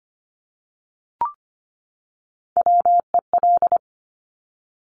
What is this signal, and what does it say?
Morse code, heard as a keyed tone, about 25 wpm, 700 Hz. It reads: WEL
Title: Top Three Letter Combinations in Words 25wpm